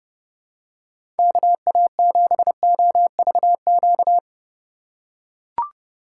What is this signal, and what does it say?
Morse code, heard as a keyed tone, about 30 wpm, 700 Hz. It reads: KA7OVQ
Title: USA Call Signs - 6 Characters - 3x Character Spacing - 30wpm